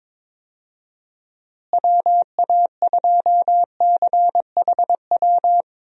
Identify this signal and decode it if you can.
Morse code, heard as a keyed tone, about 22 wpm, 700 Hz. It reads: WA2CHW
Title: USA Call Signs - 6 Characters - Repeated 3x - 22wpm